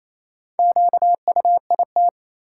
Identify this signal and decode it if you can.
Morse code, heard as a keyed tone, about 27 wpm, 700 Hz. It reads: QUIT
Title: Four Letter Words - 28wpm